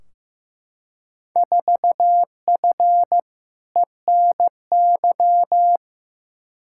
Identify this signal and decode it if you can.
Morse code, heard as a keyed tone, about 15 wpm, 700 Hz. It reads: 4F ENY
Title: Field Day Exchanges - 15wpm